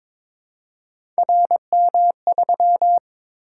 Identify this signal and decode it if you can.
Morse code, heard as a keyed tone, about 21 wpm, 700 Hz. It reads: RM3
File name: practice / 22wpm / Three Letter-Number Combinations - 22wpm